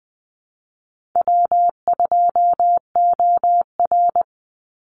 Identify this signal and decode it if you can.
Morse code, heard as a keyed tone, about 20 wpm, 700 Hz. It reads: W2OR
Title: Easy Call Signs - 20wpm